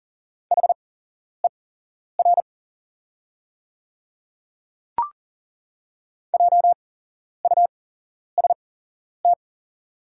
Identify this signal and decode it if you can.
Morse code, heard as a keyed tone, about 40 wpm, spaced as for 11 wpm, 700 Hz. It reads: HER JUST
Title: 8x Character Spacing - Top 200 Words - 40wpm